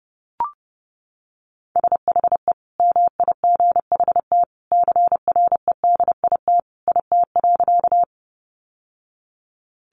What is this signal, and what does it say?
Morse code, heard as a keyed tone, about 30 wpm, 700 Hz. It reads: SHE MIGHT CREDIT IT.
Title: Sentences from Top 1000 Words 30wpm